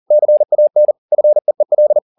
Morse code, heard as a keyed tone, about 40 wpm, 600 Hz. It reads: CAN FEEL
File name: the old man and the sea 2 40wpm